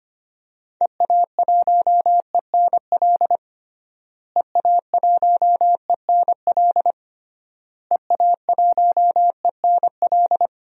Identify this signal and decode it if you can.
Morse code, heard as a keyed tone, about 25 wpm, 700 Hz. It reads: EA1ENL EA1ENL EA1ENL
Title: DXCC-100 Call Signs - 6 Characters - Repeated 3x - 25wpm